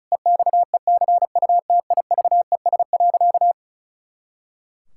Morse code, heard as a keyed tone, about 34 wpm, 700 Hz. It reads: EXECUTIVES.
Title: Pangram Challenge v2 - 35wpm